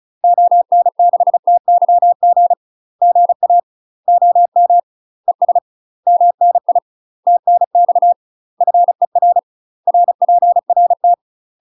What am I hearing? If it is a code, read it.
Morse code, heard as a keyed tone, about 34 wpm, 700 Hz. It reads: ON6TYG GA OM ES MNI TNX FER RPRT
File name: UK MASTER QSOs TRAINING / 35WPM / UK MASTER QSOs 0006